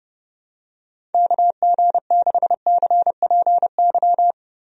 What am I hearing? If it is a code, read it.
Morse code, heard as a keyed tone, about 30 wpm, 700 Hz. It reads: KG6CPY
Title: USA Call Signs - 6 Characters - 30wpm